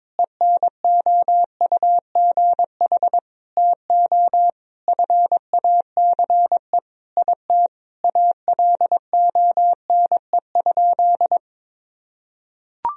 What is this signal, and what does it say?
Morse code, heard as a keyed tone, about 22 wpm, 700 Hz. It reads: ENOUGH TO FACE IT ALONE?
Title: Sentences from Top 1000 Words - Review Part 3 - 22wpm